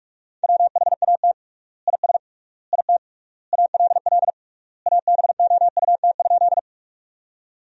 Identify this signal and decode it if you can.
Morse code, heard as a keyed tone, about 46 wpm, 700 Hz. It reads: WHAT IS IT ALL ABOUT?
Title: 3x Word-Spacing - Sentences from Top 100 Words - 45wpm